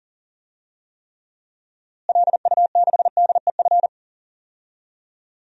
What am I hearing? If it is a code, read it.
Morse code, heard as a keyed tone, about 40 wpm, 700 Hz. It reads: LU6DEF
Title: DXCC-100 Call Signs - 6 Characters - Repeated 2x - 40wpm